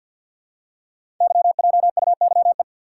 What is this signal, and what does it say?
Morse code, heard as a keyed tone, about 51 wpm, 690 Hz. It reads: YJUYE
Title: Code Groups - 5 Letters - 50wpm